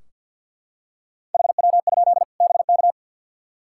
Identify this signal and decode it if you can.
Morse code, heard as a keyed tone, about 52 wpm, 700 Hz. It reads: HW? BK
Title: Practice QSO - Exchange 1 - Beginning - 50wpm